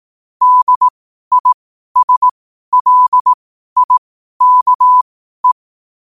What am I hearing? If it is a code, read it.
Morse code, heard as a keyed tone, about 18 wpm, spaced as for 12 wpm, 1000 Hz. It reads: DISLIKE
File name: qotdfiles / 12wpm